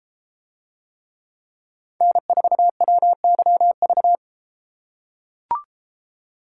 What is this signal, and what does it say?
Morse code, heard as a keyed tone, about 32 wpm, 700 Hz. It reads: N4WYV